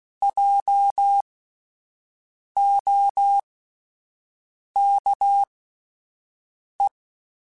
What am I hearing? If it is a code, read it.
Morse code, heard as a keyed tone, about 16 wpm, spaced as for 5 wpm, 800 Hz. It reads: JOKE
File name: Spaced Alphas / Spaced Alphas mp3s / Js sp @10